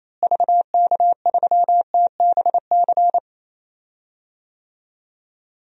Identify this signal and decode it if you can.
Morse code, heard as a keyed tone, about 28 wpm, 700 Hz. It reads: VK3TBC
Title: DXCC-100 Call Signs - 6 Characters - Repeated 3x - 28wpm